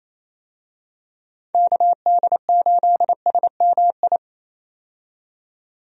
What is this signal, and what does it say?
Morse code, heard as a keyed tone, about 27 wpm, 700 Hz. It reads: KD8SMI
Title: USA Call Signs - 6 Characters - 28wpm